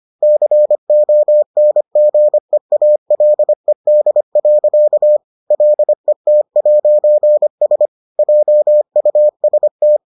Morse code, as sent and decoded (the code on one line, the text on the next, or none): -.-. --- -. --. .- .-.. -.. .-.-.- / .-.. - .----. ... / .--- ..- ... -
CONGEALED. LET'S JUST